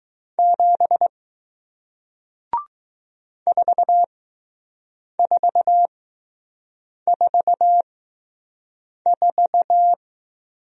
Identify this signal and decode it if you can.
Morse code, heard as a keyed tone, about 20 wpm, spaced as for 6 wpm, 700 Hz. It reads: 7 4444